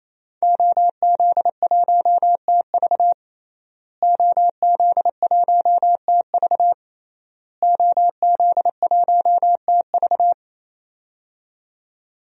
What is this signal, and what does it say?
Morse code, heard as a keyed tone, about 27 wpm, 700 Hz. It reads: OZ1TV OZ1TV OZ1TV
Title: DXCC-100 Call Signs - 5 Characters - Repeated 3x - 28wpm